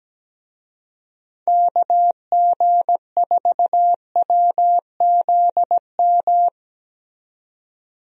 Morse code, heard as a keyed tone, about 17 wpm, 700 Hz. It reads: KG4WZM
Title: USA Call Signs - 6 Characters - 17wpm